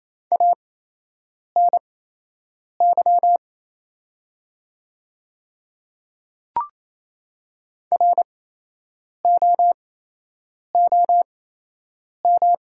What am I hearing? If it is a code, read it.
Morse code, heard as a keyed tone, about 28 wpm, spaced as for 8 wpm, 700 Hz. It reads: ANY ROOM
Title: 8x Character Spacing - Top 500 Words - 28wpm